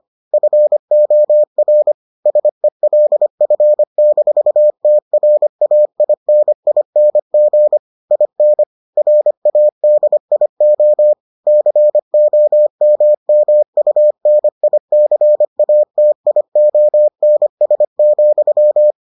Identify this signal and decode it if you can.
Morse code, heard as a keyed tone, about 25 wpm, 600 Hz. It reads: FOR SELF-TRAINING IN RADIO COMMUNICATIONS,